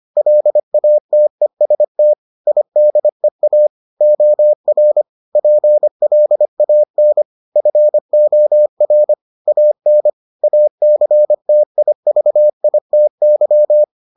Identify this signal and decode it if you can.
Morse code, heard as a keyed tone, about 25 wpm, 600 Hz. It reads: LATEST IDEA OR PLAN FOR AN ACTIVITY